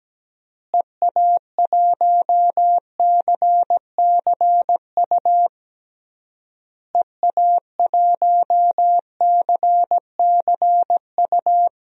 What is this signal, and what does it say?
Morse code, heard as a keyed tone, about 17 wpm, 700 Hz. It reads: EA1CCU EA1CCU
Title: DXCC-100 Call Signs - 6 Characters - Repeated 3x - 17wpm